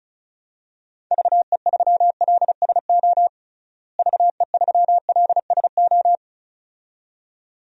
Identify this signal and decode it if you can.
Morse code, heard as a keyed tone, about 34 wpm, 700 Hz. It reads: VE3LSO VE3LSO
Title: DXCC-100 Call Signs - 6 Characters - Repeated 2x - 35wpm